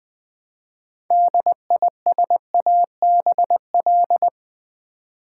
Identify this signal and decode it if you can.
Morse code, heard as a keyed tone, about 20 wpm, 700 Hz. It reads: DISABL
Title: Build A Word - 3 Letters At A Time - 8-Letter Words - 20wpm